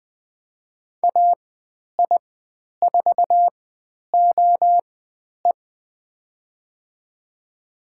Morse code, heard as a keyed tone, about 20 wpm, spaced as for 10 wpm, 700 Hz. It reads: AI4OE
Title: 109 Lesson 18g - Letter m - Call Signs 20wpm